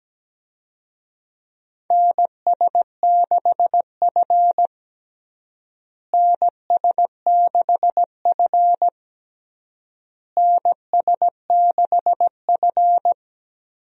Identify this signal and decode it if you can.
Morse code, heard as a keyed tone, about 17 wpm, 700 Hz. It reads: NS6F NS6F NS6F
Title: USA Call Signs - 4 Characters - Repeated 3x - 17wpm